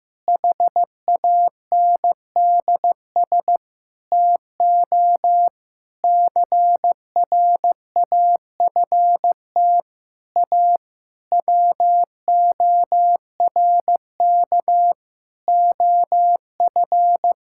Morse code, heard as a keyed tone, about 15 wpm, 700 Hz. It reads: HANDS TO CRAFT A WORK OF
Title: Sentences from Top 1000 Words - Review Part 6 - 15wpm